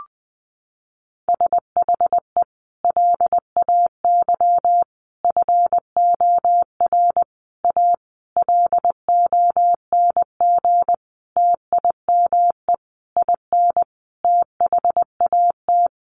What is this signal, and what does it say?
Morse code, heard as a keyed tone, about 20 wpm, 700 Hz. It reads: SHE LAY FOR A LONG TIME IN THAT
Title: Sentences from Top 900 Words 20wpm